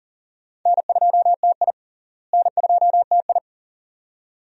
Morse code, heard as a keyed tone, about 37 wpm, 700 Hz. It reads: N2TI N2TI
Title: USA Call Signs - 4 Characters - Repeated 2x - 40wpm